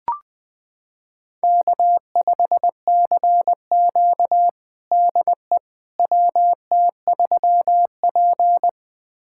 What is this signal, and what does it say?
Morse code, heard as a keyed tone, about 20 wpm, 700 Hz. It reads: K5CQ DE WT3P
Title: USA Callsign DE Callsign - 4 Characters - 20wpm